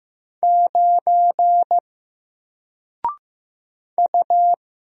Morse code, heard as a keyed tone, about 15 wpm, 700 Hz. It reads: 9 U
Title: Single Character (Letter:Number:Punctuation:Prosign) 15wpm